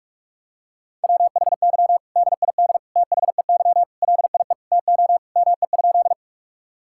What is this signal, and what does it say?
Morse code, heard as a keyed tone, about 43 wpm, 700 Hz. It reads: WHY DID THEY LIE TO ME?
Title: Sentences from Top 1600 Words 45wpm